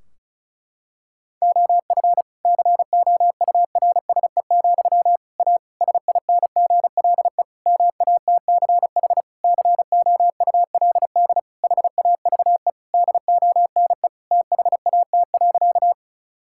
Morse code, heard as a keyed tone, about 36 wpm, 700 Hz. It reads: OF COURSE, A SINGLE MATCH COULD HAVE DONE THAT.
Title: Sentences from Top 1600 Words 35wpm